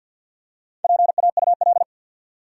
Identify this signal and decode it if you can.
Morse code, heard as a keyed tone, about 53 wpm, 700 Hz. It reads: PAUL
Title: USA Common Names - Max 5 Letters - 50wpm